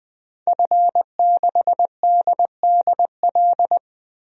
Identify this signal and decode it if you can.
Morse code, heard as a keyed tone, about 20 wpm, 700 Hz. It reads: F6DDL
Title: DXCC-100 Call Signs - 5 Characters - Repeated 3x - 20wpm